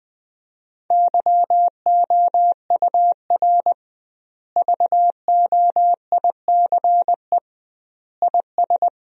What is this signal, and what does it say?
Morse code, heard as a keyed tone, about 20 wpm, 700 Hz. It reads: YOUR VOICE IS
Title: 2x Word-Spacing - Sentences from Top 500 Words - 20wpm